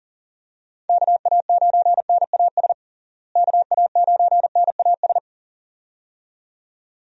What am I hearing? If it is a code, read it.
Morse code, heard as a keyed tone, about 40 wpm, 700 Hz. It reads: KA9NAS KA9NAS
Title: USA Call Signs - 6 Characters - Repeated 2x - 40wpm